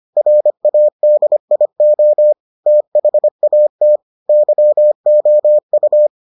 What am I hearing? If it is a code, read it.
Morse code, heard as a keyed tone, about 25 wpm, 600 Hz. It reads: RADIO THAT YOU